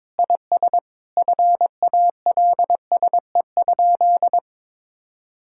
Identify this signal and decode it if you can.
Morse code, heard as a keyed tone, about 22 wpm, 700 Hz. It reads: IS FALSE?
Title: Sentences from Top 1600 Words 22wpm